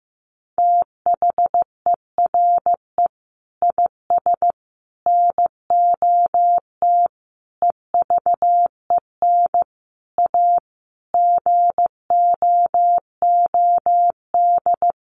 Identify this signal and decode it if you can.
Morse code, heard as a keyed tone, about 14 wpm, 700 Hz. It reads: THERE IS NOT EVEN A GOOD
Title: Sentences from Top 1500 Words 15wpm